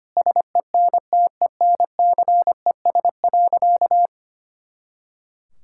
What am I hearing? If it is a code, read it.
Morse code, heard as a keyed tone, about 25 wpm, 700 Hz. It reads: SENTENCES.
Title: Pangram Challenge v2 - 25wpm